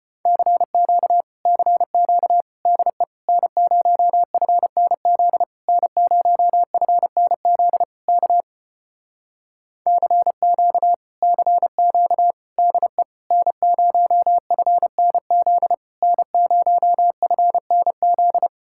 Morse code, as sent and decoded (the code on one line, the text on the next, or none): -.-. --.- / -.-. --.- / -.. / -. ----- ..-. -. --.. / -. ----- ..-. -. --.. / -.- / -.-. --.- / -.-. --.- / -.. / -. ----- ..-. -. --.. / -. ----- ..-. -. --..
CQ CQ DE N0FNZ N0FNZ K CQ CQ DE N0FNZ N0FNZ